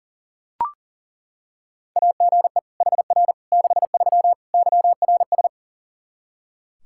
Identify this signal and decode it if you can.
Morse code, heard as a keyed tone, about 40 wpm, 700 Hz. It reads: AGE HR 63 YRS
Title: Practice QSO - Exchange 3 - More Info - 40wpm